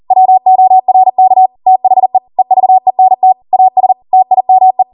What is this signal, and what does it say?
Morse code, heard as a keyed tone, about 40 wpm, 750 Hz. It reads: WORK THE EVENT AS TIME